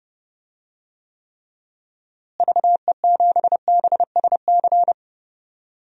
Morse code, heard as a keyed tone, about 31 wpm, 700 Hz. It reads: VE7BSC
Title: DXCC-100 Call Signs - 6 Characters - Repeated 2x - 30wpm